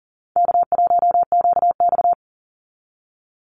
Morse code, heard as a keyed tone, about 40 wpm, 700 Hz. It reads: K1QX
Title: Easy Call Signs - 40wpm